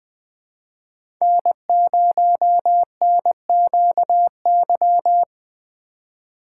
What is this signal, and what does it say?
Morse code, heard as a keyed tone, about 20 wpm, 700 Hz. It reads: N0NQY
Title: USA Call Signs - 5 Characters - 3x Character Spacing - 20wpm